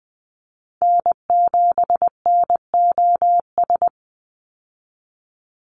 Morse code, heard as a keyed tone, about 20 wpm, 700 Hz. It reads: N7NOS
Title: USA Call Signs 20wpm